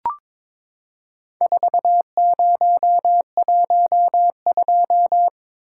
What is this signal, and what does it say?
Morse code, heard as a keyed tone, about 22 wpm, 700 Hz. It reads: 4012
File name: practice / 22wpm / Four Letter-Number Combinations - 22wpm